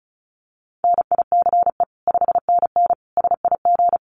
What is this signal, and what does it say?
Morse code, heard as a keyed tone, about 34 wpm, 700 Hz. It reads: NICE 5NN SIG
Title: Common QSO Phrases 35wpm